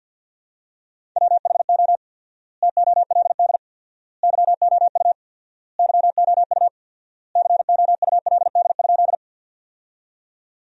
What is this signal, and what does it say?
Morse code, heard as a keyed tone, about 50 wpm, 700 Hz. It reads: WHO TOLD YOU YOU COULD?
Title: 4x Word-Spacing - Sentences from Top 400 Words - 50wpm